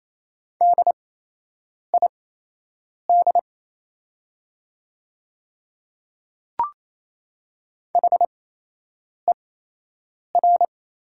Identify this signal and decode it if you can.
Morse code, heard as a keyed tone, about 28 wpm, spaced as for 8 wpm, 700 Hz. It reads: DID HER
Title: 8x Character Spacing - Top 100 Words - 28wpm